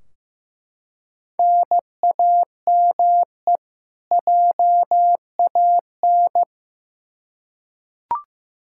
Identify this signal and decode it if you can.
Morse code, heard as a keyed tone, about 15 wpm, 700 Hz. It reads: NAME JAN
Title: Practice QSO - Exchange 1 - Beginning - 15wpm